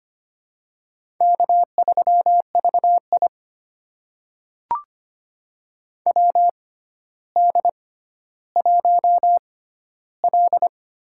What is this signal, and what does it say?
Morse code, heard as a keyed tone, about 25 wpm, spaced as for 13 wpm, 700 Hz. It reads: K3VI WD1L
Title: USA Call Signs - 4 Characters - 6x Character Spacing - 25wpm